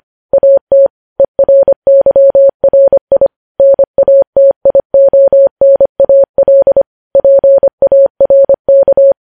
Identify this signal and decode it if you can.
Morse code, heard as a keyed tone, about 25 wpm, 550 Hz. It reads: AT ERYRI NATIONAL PARK
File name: podcast / News headlines-1x-25x25